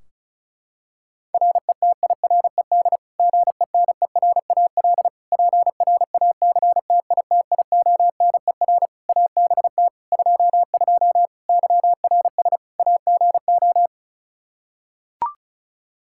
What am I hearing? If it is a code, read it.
Morse code, heard as a keyed tone, about 36 wpm, 700 Hz. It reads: RETIRED GENERAL PRACTITIONER ABT 22 YRS AGO
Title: Practice QSO - Exchange 3 - More Info - 35wpm